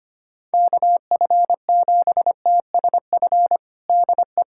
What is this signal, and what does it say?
Morse code, heard as a keyed tone, about 25 wpm, 700 Hz. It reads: KF7TSF DE
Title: USA Callsign DE Callsign - 6 Characters - 25wpm